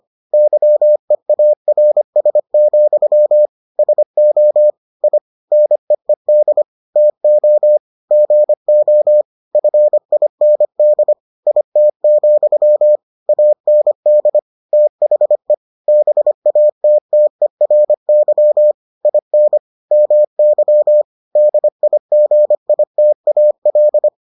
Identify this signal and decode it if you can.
Morse code, heard as a keyed tone, about 25 wpm, 600 Hz. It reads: YEARS, SO I NEED TO GO FIND IT, AND THE BATTERY IN MY DIGITAL